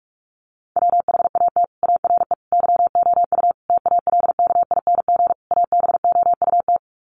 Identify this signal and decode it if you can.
Morse code, heard as a keyed tone, about 44 wpm, 700 Hz. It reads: WHAT ARE YOU TALKING ABOUT
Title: Sets of 5 Words 45wpm